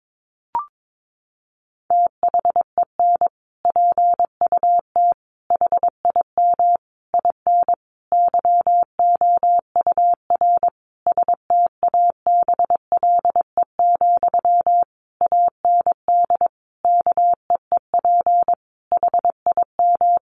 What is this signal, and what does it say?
Morse code, heard as a keyed tone, about 22 wpm, 700 Hz. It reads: THEN PUT HIM IN YOUR STABLE, AND KEEP HIM
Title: Sentences from Top 1500 Words 22wpm